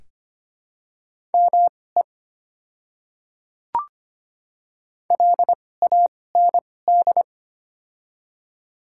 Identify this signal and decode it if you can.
Morse code, heard as a keyed tone, about 25 wpm, spaced as for 18 wpm, 700 Hz. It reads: ME LAND
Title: 2x Character Spacing - Top 300 Words - 25wpm